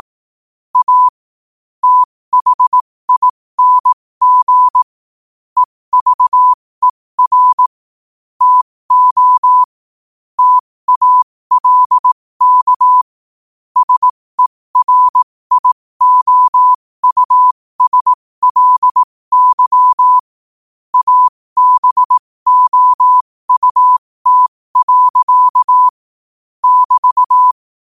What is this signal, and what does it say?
Morse code, heard as a keyed tone, about 18 wpm, spaced as for 15 wpm, 1000 Hz. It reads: A THING EVER TO TALK SERIOUSLY ABOUT. <BT>
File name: qotdfiles / 15wpm